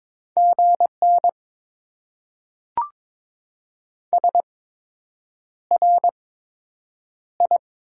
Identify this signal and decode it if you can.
Morse code, heard as a keyed tone, about 22 wpm, spaced as for 8 wpm, 700 Hz. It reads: GN SRI
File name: Top 90 QSO Elements - 8x Character Spacing - 22wpm